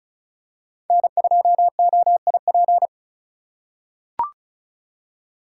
Morse code, heard as a keyed tone, about 35 wpm, 700 Hz. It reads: N2OIP
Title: USA Call Signs - 5 Characters - 3x Character Spacing - 35wpm